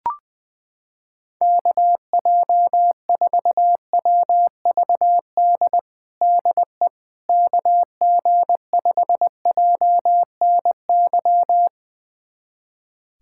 Morse code, heard as a keyed tone, about 20 wpm, 700 Hz. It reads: KJ4WVD DE KG5JNY
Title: USA Callsign DE Callsign - 6 Characters - 20wpm